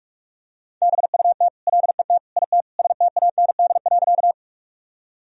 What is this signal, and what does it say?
Morse code, heard as a keyed tone, about 46 wpm, 700 Hz. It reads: BUT LET IT STAND.
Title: Sentences from Top 1000 Words - Review Part 1 - 45wpm